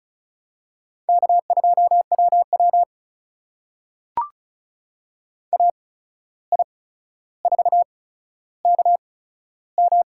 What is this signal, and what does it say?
Morse code, heard as a keyed tone, about 34 wpm, spaced as for 14 wpm, 700 Hz. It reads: K2WW AI4KM
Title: USA Call Signs - 8x Character Spacing - 35wpm